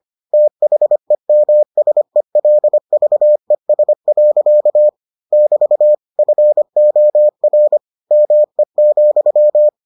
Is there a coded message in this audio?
Yes: Morse code from 0.3 to 9.7 s, heard as a keyed tone, about 25 wpm, 600 Hz. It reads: THEMSELVES. <BT> FOR ME,